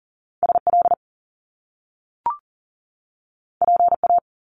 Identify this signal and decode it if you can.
Morse code, heard as a keyed tone, about 41 wpm, 700 Hz. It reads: SL PA
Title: Top Two Letter Combinations in Words 40wpm